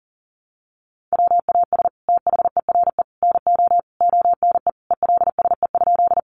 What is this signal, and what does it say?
Morse code, heard as a keyed tone, about 40 wpm, 690 Hz. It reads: WAS THERE NO ONE ELSE?
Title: Sentences from Top 800 Words 40wpm